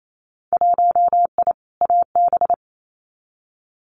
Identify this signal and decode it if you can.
Morse code, heard as a keyed tone, about 28 wpm, 700 Hz. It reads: 1I AB
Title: Winter Field Day Exchanges Encore 28wpm